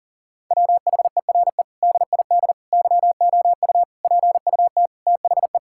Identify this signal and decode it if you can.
Morse code, heard as a keyed tone, about 37 wpm, 700 Hz. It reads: WHERE DID YOU PUT THE
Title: Sentences from Top 1000 Words - Review Part 6 - 40wpm